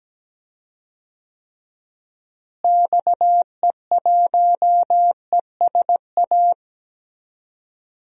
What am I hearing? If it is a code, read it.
Morse code, heard as a keyed tone, about 17 wpm, 700 Hz. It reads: XE1ESA
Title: DXCC-100 Call Signs - 6 Characters - Repeated 2x - 17wpm